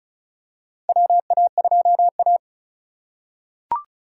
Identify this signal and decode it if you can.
Morse code, heard as a keyed tone, about 34 wpm, 700 Hz. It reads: WA2A
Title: USA Call Signs - 4 Characters - 4x Character Spacing - 35wpm